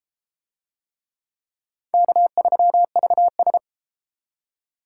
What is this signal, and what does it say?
Morse code, heard as a keyed tone, about 29 wpm, 700 Hz. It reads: K3VS